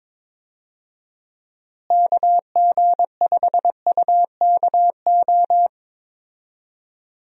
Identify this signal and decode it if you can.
Morse code, heard as a keyed tone, about 22 wpm, 700 Hz. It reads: KG5UKO
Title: USA Call Signs - 6 Characters - Repeated 3x - 22wpm